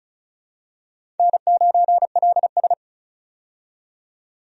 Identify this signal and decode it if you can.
Morse code, heard as a keyed tone, about 34 wpm, 700 Hz. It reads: N9LS
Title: USA Call Signs - 4 Characters - 35wpm